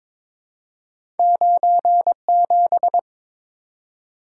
Morse code, heard as a keyed tone, about 22 wpm, 700 Hz. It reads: 97
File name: practice / 22wpm / Two Letter-Number Combinations - 22wpm